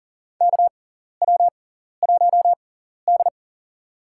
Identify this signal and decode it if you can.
Morse code, heard as a keyed tone, about 40 wpm, spaced as for 14 wpm, 700 Hz. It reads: KW1D
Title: USA Call Signs - 4 Characters - 6x Character Spacing - 40wpm